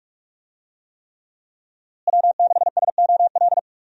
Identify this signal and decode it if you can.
Morse code, heard as a keyed tone, about 45 wpm, 700 Hz. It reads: W6SOL
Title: USA Call Signs - 5 Characters - Repeated 3x - 45wpm